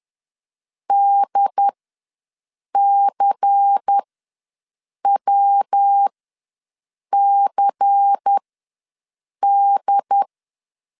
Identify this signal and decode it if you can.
Morse code, heard as a keyed tone, about 10 wpm, spaced as for 6 wpm, 800 Hz. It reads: DCWCD
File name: LEZIONI / 5 LBVC / 5 LBVC ws 10wpm 3733